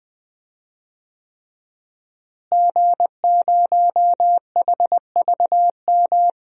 Morse code, heard as a keyed tone, about 20 wpm, 700 Hz. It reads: G0HVM
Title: DXCC-100 Call Signs - 5 Characters - Repeated 3x - 20wpm